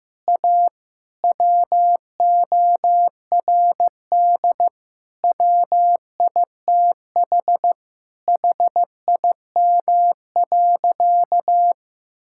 Morse code, heard as a keyed tone, about 15 wpm, 700 Hz. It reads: A WORD WITH HIM.
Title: Sentences from Top 1000 Words - Review Part 6 - 15wpm